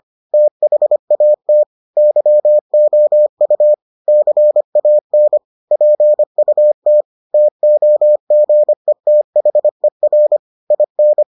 Morse code, heard as a keyed tone, about 25 wpm, 600 Hz. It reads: THAT YOU CAN PUT TOGETHER IN